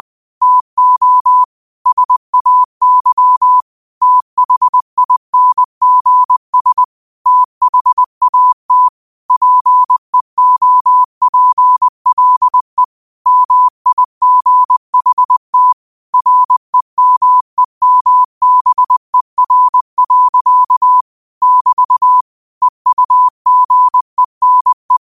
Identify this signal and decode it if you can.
Morse code, heard as a keyed tone, about 20 wpm, 1000 Hz. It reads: TO SAY THINGS THAT PEOPLE MIGHT REMEMBER. <BT> EUGENE
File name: qotdfiles / 20wpm